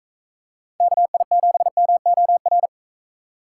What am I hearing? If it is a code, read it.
Morse code, heard as a keyed tone, about 44 wpm, 700 Hz. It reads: KI7MOR